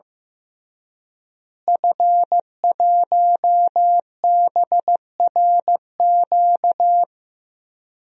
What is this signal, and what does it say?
Morse code, heard as a keyed tone, about 15 wpm, 700 Hz. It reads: F1BRQ
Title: DXCC-100 Call Signs - 5 Characters - Repeated 3x - 15wpm